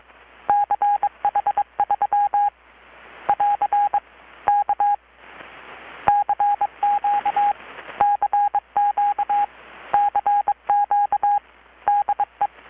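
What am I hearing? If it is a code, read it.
Morse code, heard as a keyed tone, about 22 wpm, 820 Hz. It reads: CH3 <AR> K CQ CQ CQ DE
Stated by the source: Morse from KPH's Night of Nights XX maritime radiotelegraphy event, received in Chiapas, Mexico about 3,800 km away on a Kenwood R-600 with a grounded 10 m long-wire antenna.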